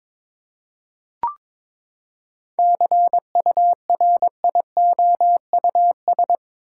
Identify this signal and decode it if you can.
Morse code, heard as a keyed tone, about 21 wpm, 700 Hz. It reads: CURIOUS
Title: Seven Letter Words - 22wpm